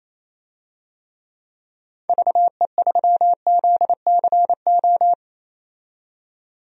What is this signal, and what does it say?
Morse code, heard as a keyed tone, about 28 wpm, 700 Hz. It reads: VE3ZCO
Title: DXCC-100 Call Signs - 6 Characters - 28wpm